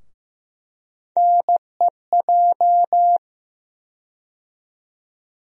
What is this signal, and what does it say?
Morse code, heard as a keyed tone, about 15 wpm, 700 Hz. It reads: NEJ
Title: USA Call Sign Postfixes - 3 Letters 15wpm